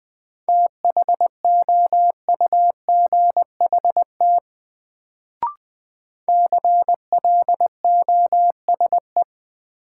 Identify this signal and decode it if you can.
Morse code, heard as a keyed tone, about 20 wpm, 700 Hz. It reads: THOUGHT CLOSE
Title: Sentences from Top 300 Words - New Word Drill - 20wpm